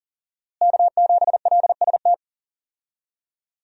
Morse code, heard as a keyed tone, about 40 wpm, 700 Hz. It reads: K7LST
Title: USA Call Signs - 5 Characters - Repeated 2x - 40wpm